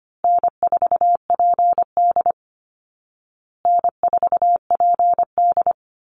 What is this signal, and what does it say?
Morse code, heard as a keyed tone, about 25 wpm, 700 Hz. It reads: N4PD N4PD
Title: Easy Call Signs Repeated 3x - 25wpm